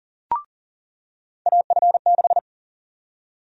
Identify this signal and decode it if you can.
Morse code, heard as a keyed tone, about 40 wpm, 700 Hz. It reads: AF6
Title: USA Call Sign Prefixes - 40wpm